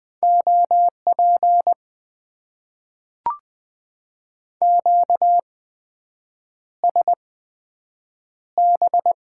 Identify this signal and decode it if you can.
Morse code, heard as a keyed tone, about 20 wpm, spaced as for 7 wpm, 700 Hz. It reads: OP QSB
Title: Top 90 QSO Elements - 8x Character Spacing - 20wpm